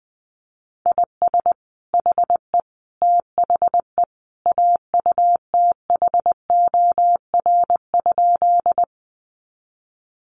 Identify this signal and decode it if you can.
Morse code, heard as a keyed tone, about 20 wpm, 700 Hz. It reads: IS HE THE AUTHOR?
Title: Sentences from Top 1200 Words 20wpm